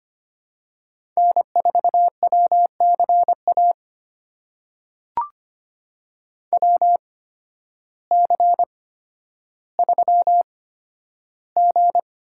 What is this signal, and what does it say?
Morse code, heard as a keyed tone, about 25 wpm, spaced as for 12 wpm, 700 Hz. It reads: N4WCA WC3G
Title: USA Call Signs - 8x Character Spacing - 25wpm